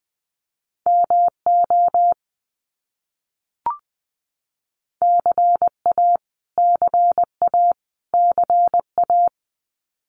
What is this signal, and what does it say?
Morse code, heard as a keyed tone, about 20 wpm, 700 Hz. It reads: MO CA CA CA
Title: US States Repeated 3x 20wpm